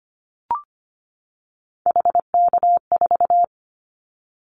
Morse code, heard as a keyed tone, about 25 wpm, 700 Hz. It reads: HK4
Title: Callsign Prefixes for Top 100 DXCC - 25wpm